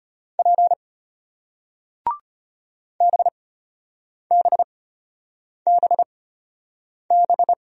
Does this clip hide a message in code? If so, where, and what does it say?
Yes: Morse code from 0.4 to 7.5 s, heard as a keyed tone, about 33 wpm, spaced as for 8 wpm, 700 Hz. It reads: P BBBB